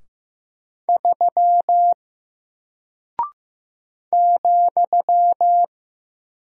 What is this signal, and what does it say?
Morse code, heard as a keyed tone, about 15 wpm, 700 Hz. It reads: 3 ,
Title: Single Character (Letter:Number:Punctuation:Prosign) Alt 15wpm